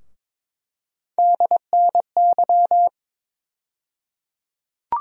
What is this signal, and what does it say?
Morse code, heard as a keyed tone, about 22 wpm, 700 Hz. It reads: DNY
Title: USA Call Sign Postfixes - 3 Letters 22wpm